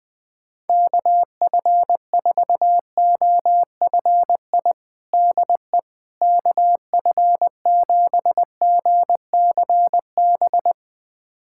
Morse code, heard as a keyed tone, about 20 wpm, 700 Hz. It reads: KF4OFI DE KF7GCB
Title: USA Callsign DE Callsign - 6 Characters - 20wpm